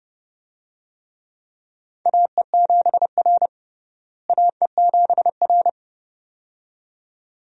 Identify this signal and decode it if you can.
Morse code, heard as a keyed tone, about 30 wpm, 700 Hz. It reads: AE7R AE7R
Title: USA Call Signs - 4 Characters - Repeated 2x - 30wpm